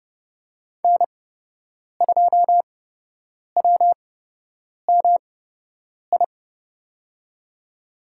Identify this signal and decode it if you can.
Morse code, heard as a keyed tone, about 31 wpm, spaced as for 8 wpm, 700 Hz. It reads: N2WMI
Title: USA Call Signs - 5 Characters - 8x Character Spacing - 30wpm